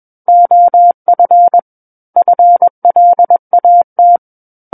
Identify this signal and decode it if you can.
Morse code, heard as a keyed tone, about 21 wpm, 700 Hz. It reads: OF FLAT
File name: Book One / 12-Chapter 12 - 21 WPM